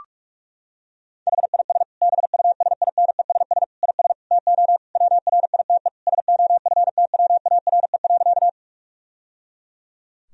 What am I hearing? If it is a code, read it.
Morse code, heard as a keyed tone, about 44 wpm, 700 Hz. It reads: HIS BUSINESS IS TO WRITE SOFTWARE.
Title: Sentences from Top 1000 Words - Review Part 3 - 45wpm